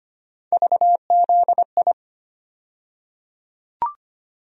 Morse code, heard as a keyed tone, about 25 wpm, 700 Hz. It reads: VZI
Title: USA Call Sign Postfixes - 3 Letters 25wpm